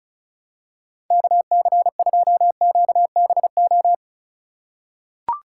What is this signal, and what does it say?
Morse code, heard as a keyed tone, about 36 wpm, 700 Hz. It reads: KC2QBO
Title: USA Call Signs - 6 Characters - 3x Character Spacing - 35wpm